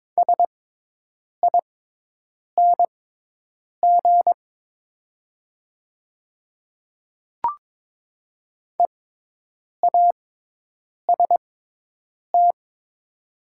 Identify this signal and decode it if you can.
Morse code, heard as a keyed tone, about 22 wpm, spaced as for 8 wpm, 700 Hz. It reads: SING EAST
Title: 6x Character Spacing - Top 500 Words - 22wpm